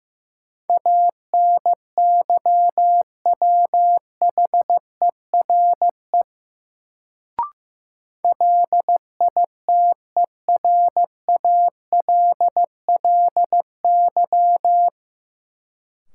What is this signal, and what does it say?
Morse code, heard as a keyed tone, about 15 wpm, 700 Hz. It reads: ANYWHERE LITERALLY
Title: Sentences from Top 1400 Words - New Word Drill - 15wpm